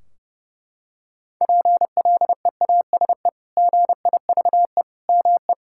Morse code, heard as a keyed tone, about 30 wpm, 700 Hz. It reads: PLEASE GIVE ME
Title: Sentences from Top 1600 Words 30wpm